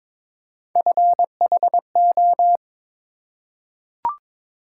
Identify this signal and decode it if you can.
Morse code, heard as a keyed tone, about 22 wpm, 700 Hz. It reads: FHO
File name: practice / 22wpm / Three Letter-Number Combinations - 22wpm